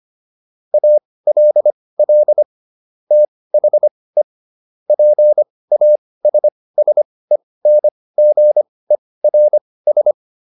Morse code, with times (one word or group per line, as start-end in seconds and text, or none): 0.7-2.4 ALL
3.1-4.2 THE
4.9-10.1 PASSENGERS